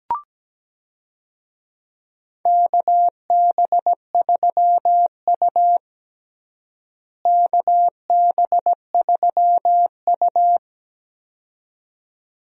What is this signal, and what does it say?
Morse code, heard as a keyed tone, about 17 wpm, 700 Hz. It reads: KB3U KB3U
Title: USA Call Signs - 4 Characters - Repeated 2x - 17wpm